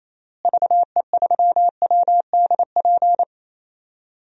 Morse code, heard as a keyed tone, about 28 wpm, 700 Hz. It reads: VE3WDP
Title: DXCC-100 Call Signs - 6 Characters - Repeated 3x - 28wpm